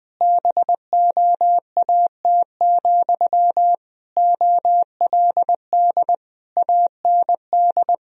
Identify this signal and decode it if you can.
Morse code, heard as a keyed tone, about 19 wpm, 700 Hz. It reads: BOAT, OLD AND
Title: Sentences from Top 1000 Words - Review Part 3 - 20wpm